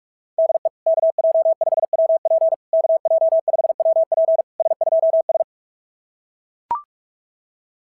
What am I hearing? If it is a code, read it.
Morse code, heard as a keyed tone, about 41 wpm, 660 Hz. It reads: DE KJ5WP KJ5WP S2S
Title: SOTA QSO Practice 45wpm